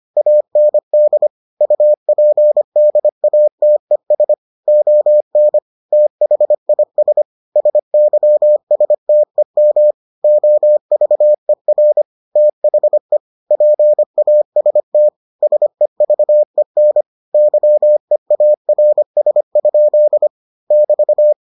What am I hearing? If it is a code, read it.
Morse code, heard as a keyed tone, about 25 wpm, 600 Hz. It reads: AND UPDATES ON THIS SYSTEM OVER THE PAST SEVEN YEARS? <BT>